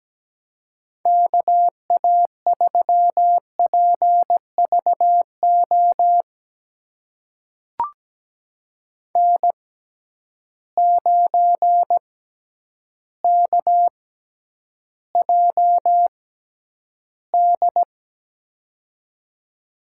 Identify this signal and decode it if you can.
Morse code, heard as a keyed tone, about 17 wpm, spaced as for 9 wpm, 700 Hz. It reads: KA3PVO N9KJD
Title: USA Call Signs - 6x Character Spacing - 17wpm